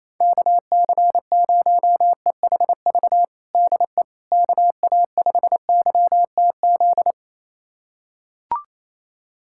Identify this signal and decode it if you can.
Morse code, heard as a keyed tone, about 28 wpm, 700 Hz. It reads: KC0EHV DE KA5YTZ
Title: USA Callsign DE Callsign - 6 Characters - 28wpm